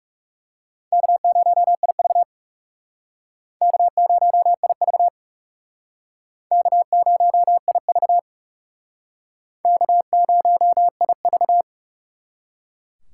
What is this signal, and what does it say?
Morse code, heard as a keyed tone, about 37 wpm, 700 Hz. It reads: K0IV K0IV K0IV K0IV
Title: Speed Racing - USA 1x2 Call Signs 30wpm